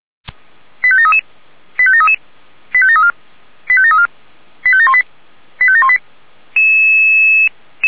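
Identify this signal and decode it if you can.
Five-tone selective call (ZVEI), heard as quick runs of five tones, tones of about 70 ms. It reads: 85733 85733 85734 85734 85718 85718
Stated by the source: Five-tone ZVEI selective call received on an SDR.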